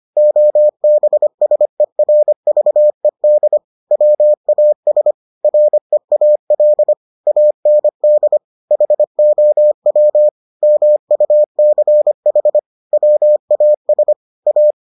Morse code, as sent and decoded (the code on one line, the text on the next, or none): --- -... ... .-. ...- . -.. / .-- .- ... / .-. .- .-.. / .- -. -.. / .... --- .-- / -- ..- -.-. .... / .-- .- ... / .-
OBSERVED WAS REAL AND HOW MUCH WAS A